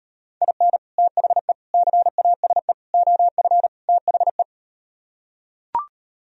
Morse code, heard as a keyed tone, about 38 wpm, 700 Hz. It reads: IN THE CASE OF THE